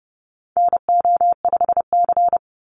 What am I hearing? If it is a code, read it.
Morse code, heard as a keyed tone, about 30 wpm, 700 Hz. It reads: NO5C
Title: USA Call Signs 30wpm